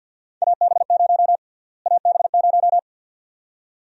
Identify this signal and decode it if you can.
Morse code, heard as a keyed tone, about 50 wpm, 700 Hz. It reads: AB0 AB0
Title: USA Call Sign Prefixes - Repeated 2x - 50wpm